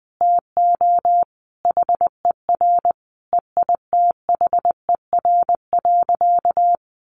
Morse code, heard as a keyed tone, about 20 wpm, 700 Hz. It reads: TO HER EITHER.
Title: Sentences from Top 1000 Words 20wpm